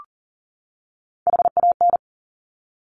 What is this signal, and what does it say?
Morse code, heard as a keyed tone, about 42 wpm, 700 Hz. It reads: HAN
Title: Top Three Letter Combinations in Words 40wpm